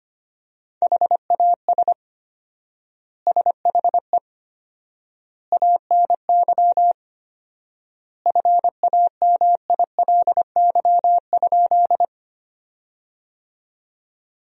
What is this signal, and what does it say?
Morse code, heard as a keyed tone, about 25 wpm, 700 Hz. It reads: HAS SHE ANY FAMILY?
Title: 4x Word-Spacing - Sentences from Top 400 Words - 25wpm